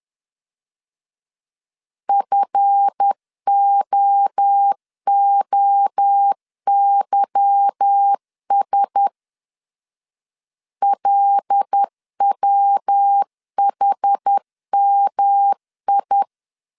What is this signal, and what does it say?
Morse code, heard as a keyed tone, about 10 wpm, 800 Hz. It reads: FOOYS LWHMI
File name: CW / DET / 1 9 1500C S A 10wpm 4511